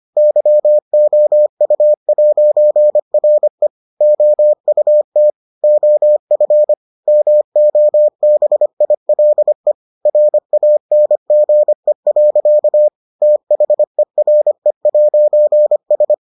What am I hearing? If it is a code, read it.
Morse code, heard as a keyed tone, about 25 wpm, 600 Hz. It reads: YOU'RE OUT OF MOBILE RANGE. THERE'S